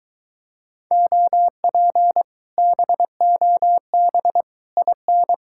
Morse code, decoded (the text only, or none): OP BOB IN